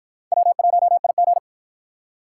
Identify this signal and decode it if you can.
Morse code, heard as a keyed tone, about 53 wpm, 700 Hz. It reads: W1IG